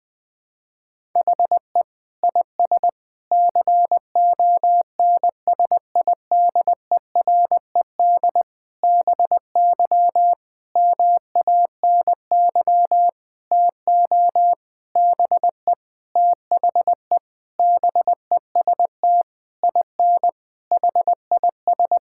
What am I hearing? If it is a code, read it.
Morse code, heard as a keyed tone, about 20 wpm, 700 Hz. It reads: HE IS CONSIDERED BY MANY TO BE THE BEST IN HIS
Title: Sentences with 7-Word Phrases - Spoken Beginning - 20wpm